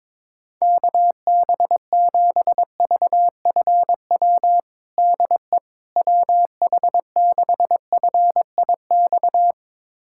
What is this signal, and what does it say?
Morse code, heard as a keyed tone, about 22 wpm, 700 Hz. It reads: KB7VFW DE WH6FIX
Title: USA Callsign DE Callsign - 6 Characters - 22wpm